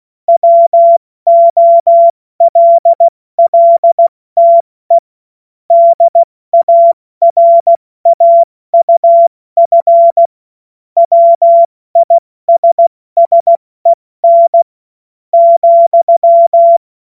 Morse code, heard as a keyed tone, about 16 wpm, spaced as for 14 wpm, 680 Hz. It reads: WOLLTE DARAUF WISSEN,
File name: Morsekurs / Witze / Witz 30. August 2022 70 F80